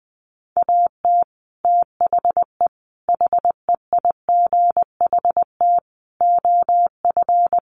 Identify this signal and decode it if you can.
Morse code, heard as a keyed tone, about 20 wpm, 700 Hz. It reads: AT THE HEIGHT OF
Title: Sets of 5 Words 20wpm